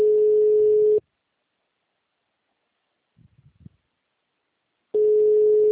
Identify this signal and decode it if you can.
Ringback tone (Europe, 425 Hz).
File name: out- -2011- - - .43506